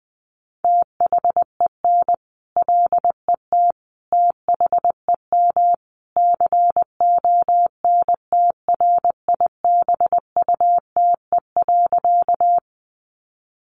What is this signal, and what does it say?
Morse code, heard as a keyed tone, about 20 wpm, 700 Hz. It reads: THEN LET THEM CONTRIBUTE.
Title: Sentences from Top 1400 Words 20wpm